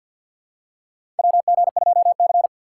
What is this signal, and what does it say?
Morse code, heard as a keyed tone, about 51 wpm, 700 Hz. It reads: WG2C
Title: USA Call Signs - 2x Character Spacing - 50wpm